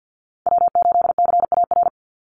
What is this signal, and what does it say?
Morse code, heard as a keyed tone, about 48 wpm, 700 Hz. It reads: W8CAR
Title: Call Signs Repeated 3x - 50wpm